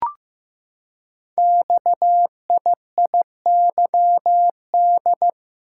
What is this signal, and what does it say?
Morse code, heard as a keyed tone, about 15 wpm, 700 Hz. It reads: XIIYD
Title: Code Groups - 5 Letter:Numbers - 15wpm